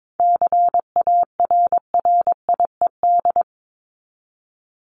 Morse code, heard as a keyed tone, about 22 wpm, 700 Hz. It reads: CARRIED
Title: ED Suffix - Top 100 Words with Base Words 22wpm